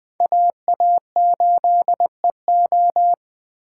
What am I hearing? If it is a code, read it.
Morse code, heard as a keyed tone, about 20 wpm, 700 Hz. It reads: AA8EO
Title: Speed Racing - USA 2x2 Call Signs 17wpm